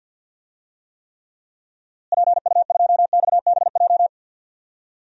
Morse code, heard as a keyed tone, about 52 wpm, 700 Hz. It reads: PU2XBJ
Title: DXCC-100 Call Signs - 6 Characters - Repeated 2x - 50wpm